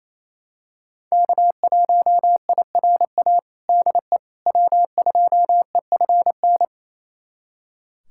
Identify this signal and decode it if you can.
Morse code, heard as a keyed tone, about 27 wpm, 700 Hz. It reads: K1IRA DE W2EFN
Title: USA Callsign DE Callsign - 5 Characters - 28wpm